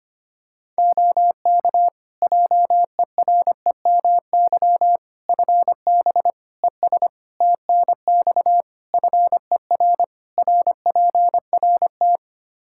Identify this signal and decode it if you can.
Morse code, heard as a keyed tone, about 25 wpm, 700 Hz. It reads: OK JEREMY FB ES TNX FER RPRT
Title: Practice QSO - Exchange 2 - Info - 25wpm